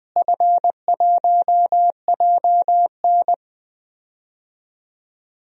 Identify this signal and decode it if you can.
Morse code, heard as a keyed tone, about 20 wpm, 700 Hz. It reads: F1JN
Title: DXCC-100 Call Signs - 4 Characters - Repeated 2x - 20wpm